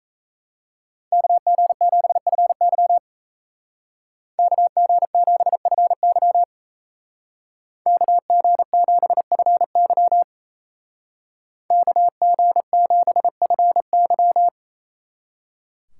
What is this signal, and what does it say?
Morse code, heard as a keyed tone, about 34 wpm, 700 Hz. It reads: KG7FY KG7FY KG7FY KG7FY
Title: Speed Racing - USA 2x2 Call Signs 28wpm